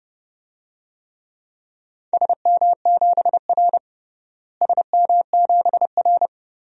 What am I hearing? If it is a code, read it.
Morse code, heard as a keyed tone, about 32 wpm, 700 Hz. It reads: SM7R SM7R
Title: DXCC-100 Call Signs - 4 Characters - Repeated 3x - 30wpm